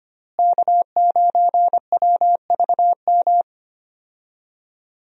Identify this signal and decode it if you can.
Morse code, heard as a keyed tone, about 25 wpm, 700 Hz. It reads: K9WVM
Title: USA Call Signs - 5 Characters - Repeated 3x - 25wpm